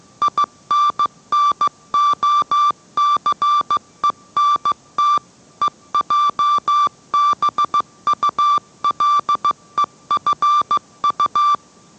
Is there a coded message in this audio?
Yes: Morse code from 0.2 to 11.6 s, heard as a keyed tone, about 15 wpm, 1170 Hz. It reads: INNOCENT EJBULEFU